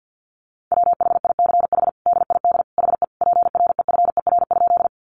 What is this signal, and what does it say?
Morse code, heard as a keyed tone, about 49 wpm, 700 Hz. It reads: WHICH DID HE PREFER?
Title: Sentences from Top 1100 Words 50wpm